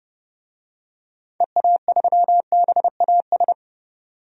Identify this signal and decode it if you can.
Morse code, heard as a keyed tone, about 32 wpm, 700 Hz. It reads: EA3BAS
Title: DXCC-100 Call Signs - 6 Characters - 30wpm